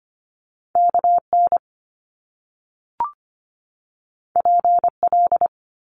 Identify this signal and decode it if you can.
Morse code, heard as a keyed tone, about 25 wpm, 700 Hz. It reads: KN PL
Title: Top Two Letter Combinations in Words 25wpm